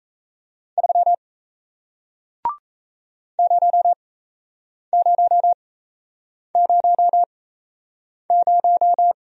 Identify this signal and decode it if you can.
Morse code, heard as a keyed tone, about 38 wpm, spaced as for 8 wpm, 700 Hz. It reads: 3 0000